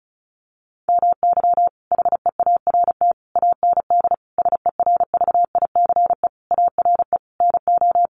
Morse code, heard as a keyed tone, about 34 wpm, 700 Hz. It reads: MY HEART AND SERVICE ARE NO
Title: Sentences from Top 1200 Words 35wpm